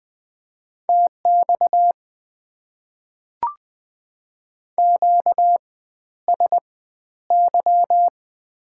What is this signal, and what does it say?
Morse code, heard as a keyed tone, about 20 wpm, spaced as for 11 wpm, 690 Hz. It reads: TX QSY
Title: Top 90 QSO Elements - 4x Character Spacing - 20wpm